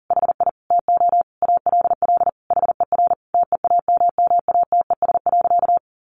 Morse code, heard as a keyed tone, about 40 wpm, 700 Hz. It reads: HI TO ALL HER TEAMMATES.
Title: Easy Phrases 40wpm